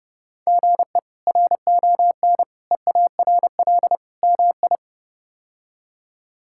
Morse code, heard as a keyed tone, about 32 wpm, 700 Hz. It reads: GE RON EARL MI
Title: K1USN SST QSO Practice - 30wpm